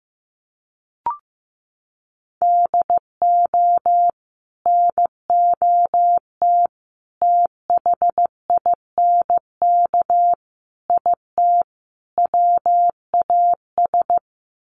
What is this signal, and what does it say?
Morse code, heard as a keyed tone, about 15 wpm, 700 Hz. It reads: DO NOT THINK IT WAS
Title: Sets of 5 Words Encore 15wpm